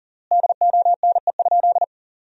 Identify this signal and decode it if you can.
Morse code, heard as a keyed tone, about 40 wpm, 700 Hz. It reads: DONE?
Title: 3x Word-Spacing - Sentences from Top 500 Words - 40wpm